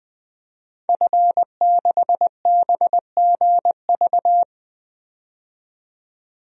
Morse code, heard as a keyed tone, about 21 wpm, 700 Hz. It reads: F6BGV
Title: DXCC-100 Call Signs - 5 Characters - 20wpm